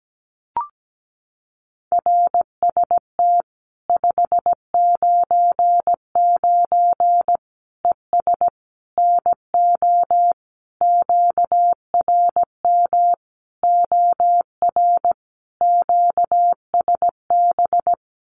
Morse code, heard as a keyed tone, about 17 wpm, 700 Hz. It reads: RST 599 ES NO QRM OR QSB
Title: Common QSO Phrases 17wpm